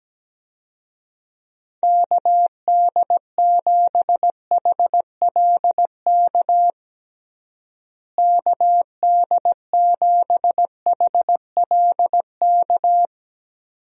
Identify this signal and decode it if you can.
Morse code, heard as a keyed tone, about 17 wpm, 700 Hz. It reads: KD7HLK KD7HLK
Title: USA Call Signs - 6 Characters - Repeated 2x - 17wpm